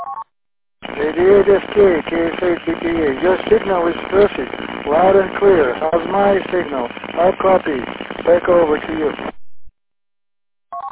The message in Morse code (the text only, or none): none